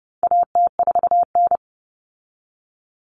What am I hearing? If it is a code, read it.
Morse code, heard as a keyed tone, about 30 wpm, 700 Hz. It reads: AT4N
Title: Common Cut Numbers - 4-Digit 30wpm